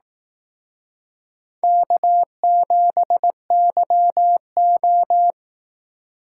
Morse code, heard as a keyed tone, about 18 wpm, 700 Hz. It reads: K7YO
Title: Speed Racing - USA 1x2 Call Signs 15wpm